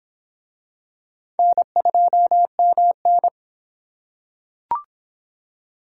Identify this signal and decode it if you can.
Morse code, heard as a keyed tone, about 24 wpm, 700 Hz. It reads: N2MN